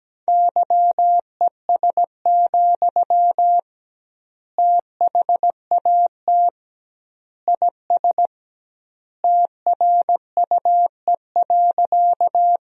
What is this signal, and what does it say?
Morse code, heard as a keyed tone, about 17 wpm, 700 Hz. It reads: YES, THAT IS TRUE.
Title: 2x Word-Spacing - Sentences from Top 500 Words - 17wpm